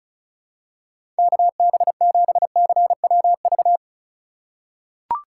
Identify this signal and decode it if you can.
Morse code, heard as a keyed tone, about 36 wpm, 700 Hz. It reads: KB7CWV
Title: USA Call Signs - 6 Characters - 2x Character Spacing - 35wpm